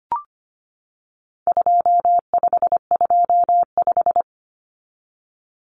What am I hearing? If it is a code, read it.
Morse code, heard as a keyed tone, about 25 wpm, 700 Hz. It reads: 2525
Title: Common Cut Numbers - 4-Digit 25wpm